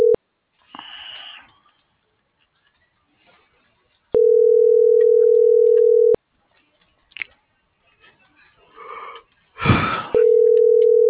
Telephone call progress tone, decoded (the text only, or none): ringback tone